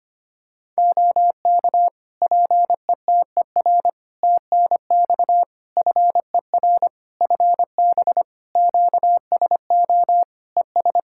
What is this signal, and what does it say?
Morse code, heard as a keyed tone, about 25 wpm, 700 Hz. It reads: OK PETER TNX FER FB QSO ES
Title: K1USN SST QSO Practice 25wpm